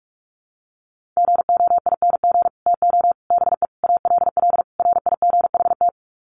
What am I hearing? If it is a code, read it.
Morse code, heard as a keyed tone, about 44 wpm, 700 Hz. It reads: GOING TO BE ALL RIGHT
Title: Sets of 5 Words Encore2 45wpm